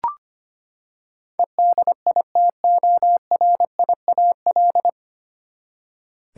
Morse code, heard as a keyed tone, about 25 wpm, 700 Hz. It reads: EDITORIAL
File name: Nine Letter Words - 25wpm